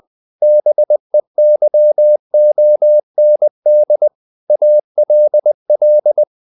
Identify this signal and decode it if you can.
Morse code, heard as a keyed tone, about 20 wpm, 600 Hz. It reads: BEYOND ALL
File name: the old man and the sea / the old man and the sea part 1 20wpm